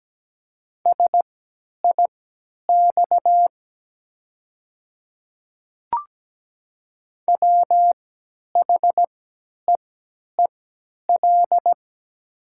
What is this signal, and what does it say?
Morse code, heard as a keyed tone, about 17 wpm, spaced as for 10 wpm, 690 Hz. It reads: SIX WHEEL
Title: 3x Character Spacing - Top 500 Words - 17wpm